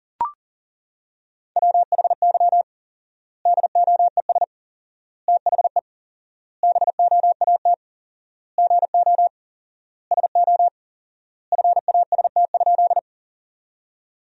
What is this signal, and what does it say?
Morse code, heard as a keyed tone, about 40 wpm, 700 Hz. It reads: WHY DOES THE BOAT GO SO FAST?
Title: 4x Word-Spacing - Sentences from Top 500 Words - 40wpm